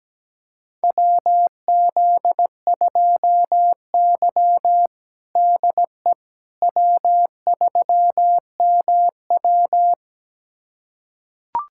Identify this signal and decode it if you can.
Morse code, heard as a keyed tone, about 17 wpm, 700 Hz. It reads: WZ2Y DE W3MW
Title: USA Callsign DE Callsign - 4 Characters - 17wpm